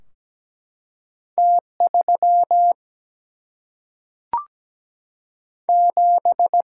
Morse code, heard as a keyed tone, about 17 wpm, 700 Hz. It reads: T3 7
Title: Single-Letter Single-Number 17wpm